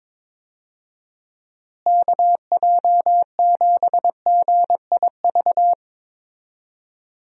Morse code, heard as a keyed tone, about 22 wpm, 700 Hz. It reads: KJ7GIV
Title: USA Call Signs - 6 Characters - Repeated 2x - 22wpm